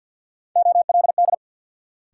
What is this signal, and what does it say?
Morse code, heard as a keyed tone, about 53 wpm, 700 Hz. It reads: OLD
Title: Sets of 2 Words - New Word Drill - 50wpm